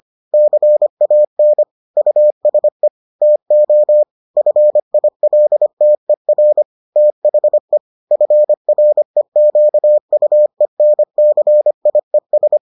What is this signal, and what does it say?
Morse code, heard as a keyed tone, about 25 wpm, 600 Hz. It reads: CAN USE TO FILTER THE FREQUENCIES